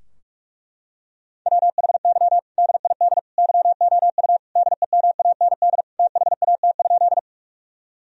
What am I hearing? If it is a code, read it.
Morse code, heard as a keyed tone, about 42 wpm, 700 Hz. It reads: WHY DID YOU DEMAND THAT?
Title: Sentences from Top 1000 Words - Review Part 4 - 45wpm